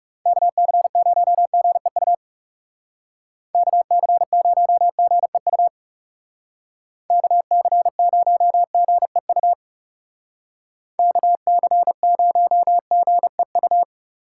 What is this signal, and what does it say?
Morse code, heard as a keyed tone, about 40 wpm, 700 Hz. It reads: KC0GEU KC0GEU KC0GEU KC0GEU